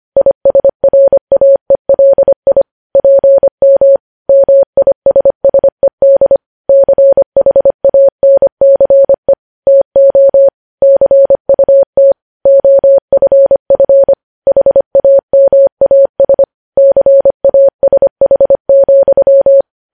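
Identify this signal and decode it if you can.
Morse code, heard as a keyed tone, about 25 wpm, 550 Hz. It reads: ISRAELI PM MISSED CHANCE TO CUT OFF HAMAS CASH,
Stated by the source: News headlines sent in Morse code at 25.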